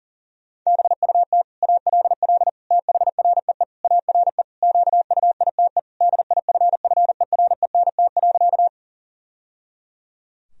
Morse code, heard as a keyed tone, about 40 wpm, 700 Hz. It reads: BUT ALL THREE ARE QUITE DIFFERENT.
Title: Sentences from Top 1000 Words - Review Part 3 - 40wpm